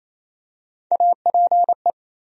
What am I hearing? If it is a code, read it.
Morse code, heard as a keyed tone, about 27 wpm, 700 Hz. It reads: APE
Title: USA Call Sign Postfixes - 3 Letters 28wpm